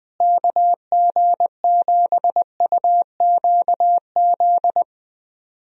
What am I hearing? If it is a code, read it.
Morse code, heard as a keyed tone, about 20 wpm, 700 Hz. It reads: KG7UQZ
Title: USA Call Signs - 6 Characters - Repeated 2x - 20wpm